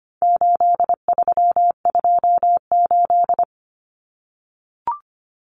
Morse code, heard as a keyed tone, about 25 wpm, 700 Hz. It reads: 8328
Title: Numbers - 4-Digit 25wpm